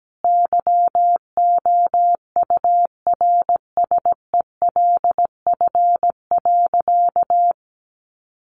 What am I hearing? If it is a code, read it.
Morse code, heard as a keyed tone, about 17 wpm, 700 Hz. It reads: YOURSELF.
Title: Sentences from Top 600 Words 17wpm